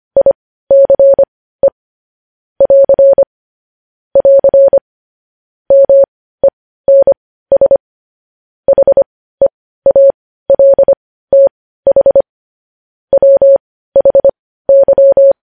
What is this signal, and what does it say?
Morse code, heard as a keyed tone, about 25 wpm, spaced as for 15 wpm, 550 Hz. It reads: ICE <AR> <AR> MENS HEALTH WHY
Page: news headlines in morse code at 15 wpm